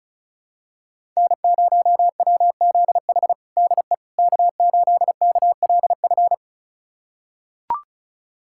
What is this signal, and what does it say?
Morse code, heard as a keyed tone, about 36 wpm, 700 Hz. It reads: N0WZH DE K8KLF
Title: USA Callsign DE Callsign - 5 Characters - 35wpm